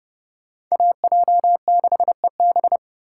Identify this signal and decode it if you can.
Morse code, heard as a keyed tone, about 38 wpm, 700 Hz. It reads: AJ6EB